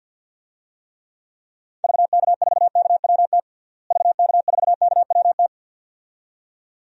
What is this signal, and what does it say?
Morse code, heard as a keyed tone, about 51 wpm, 700 Hz. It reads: VK4KWT VK4KWT
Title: DXCC-100 Call Signs - 6 Characters - Repeated 2x - 50wpm